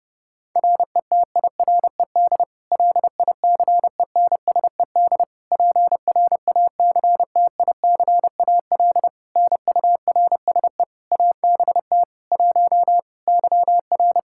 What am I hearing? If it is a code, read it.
Morse code, heard as a keyed tone, about 30 wpm, 700 Hz. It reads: RETIRED LICENSED PRACTICAL NURSE ABT 1 YR
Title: Practice QSO - Exchange 3 - More Info - 30wpm